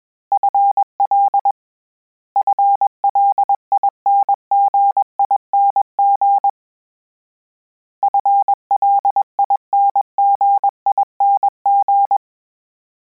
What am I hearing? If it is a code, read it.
Morse code, heard as a keyed tone, about 21 wpm, 800 Hz. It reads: FL FLINGING FLINGING
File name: Combinations / Combinations-mp3s / Combinations@20